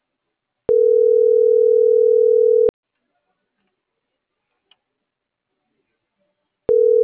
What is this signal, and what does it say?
Ringback tone (North America), 440 + 480 Hz.